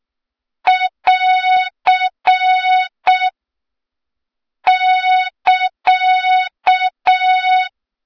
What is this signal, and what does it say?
Morse code, heard as a keyed tone, about 6 wpm, 3700 Hz, with noise. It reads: <AR> <KA>
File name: Morze-HA2MN / hang / 23 GYAKORLAT